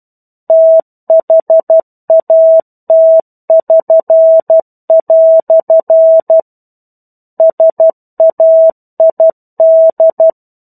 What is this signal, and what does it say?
Morse code, heard as a keyed tone, about 12 wpm, 650 Hz. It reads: THAT<SN>" SAID
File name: 01-Chapter-01 12 WPM